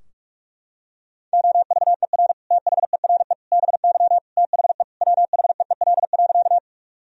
Morse code, heard as a keyed tone, about 46 wpm, 700 Hz. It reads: OVER THERE BY THE WHEEL.
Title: Sentences from Top 1000 Words - Review Part 3 - 45wpm